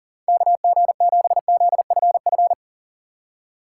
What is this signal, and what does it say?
Morse code, heard as a keyed tone, about 52 wpm, 700 Hz. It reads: KG7ZFF